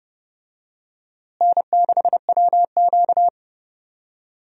Morse code, heard as a keyed tone, about 30 wpm, 700 Hz. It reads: N6WQ
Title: USA Call Signs - 4 Characters - Repeated 3x - 30wpm